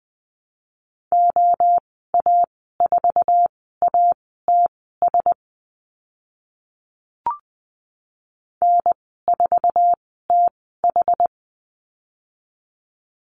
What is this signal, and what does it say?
Morse code, heard as a keyed tone, about 20 wpm, spaced as for 14 wpm, 700 Hz. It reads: OA4ATS N4TH
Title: 062 Lesson 11g - Letter h - Call Signs 20wpm